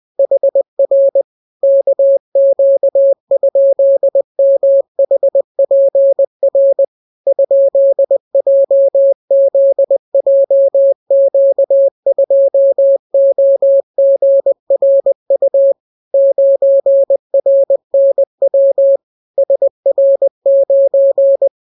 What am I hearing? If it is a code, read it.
Morse code, heard as a keyed tone, about 20 wpm, 550 Hz. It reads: HR KQ?MHPR ?JZJQ2OGRU 9RNW SR9